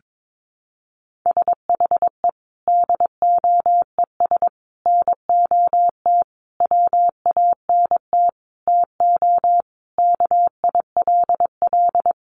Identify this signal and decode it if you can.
Morse code, heard as a keyed tone, about 22 wpm, 700 Hz. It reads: SHE DOES NOT WANT TO KILL
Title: Additional Easy Phrases 22wpm